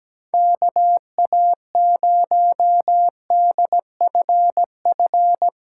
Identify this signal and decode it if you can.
Morse code, heard as a keyed tone, about 17 wpm, 700 Hz. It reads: KA0DFF
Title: USA Callsign DE Callsign 17wpm